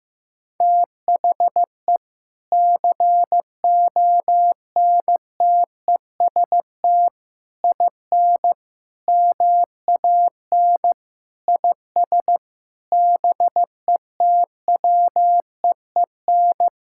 Morse code, heard as a keyed tone, about 15 wpm, 700 Hz. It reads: THE CONTEST IN MAN IS BETWEEN
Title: Sentences from Top 1600 Words 15wpm